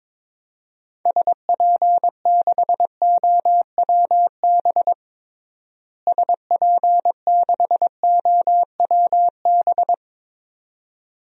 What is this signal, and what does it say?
Morse code, heard as a keyed tone, about 22 wpm, 700 Hz. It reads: SP6OWB SP6OWB
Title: DXCC-100 Call Signs - 6 Characters - Repeated 2x - 22wpm